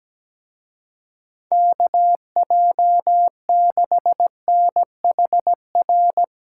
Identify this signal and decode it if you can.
Morse code, heard as a keyed tone, about 17 wpm, 700 Hz. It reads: KJ6NHR
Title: Speed Racing - USA 2x3 Call Signs 17wpm